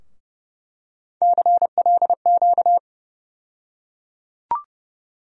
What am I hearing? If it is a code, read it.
Morse code, heard as a keyed tone, about 31 wpm, 700 Hz. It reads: CLQ
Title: USA Call Sign Postfixes - 3 Letters 30wpm